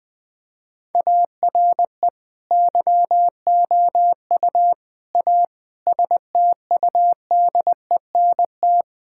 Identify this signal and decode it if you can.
Morse code, heard as a keyed tone, about 21 wpm, 700 Hz. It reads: ARE YOU A STUDENT
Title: Sentences from Top 1000 Words - Review Part 6 - 20wpm